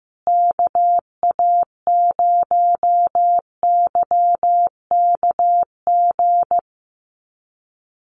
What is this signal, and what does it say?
Morse code, heard as a keyed tone, about 15 wpm, 700 Hz. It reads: KA0YKG
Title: USA Call Signs Repeated 3x 15wpm